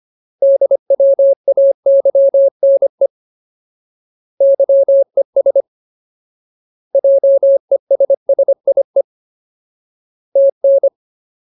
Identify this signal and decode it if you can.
Morse code, heard as a keyed tone, about 25 wpm, 550 Hz. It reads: DWAYNE YES JESSIE TN
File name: Ultimate Word List / 25WPM Ultimate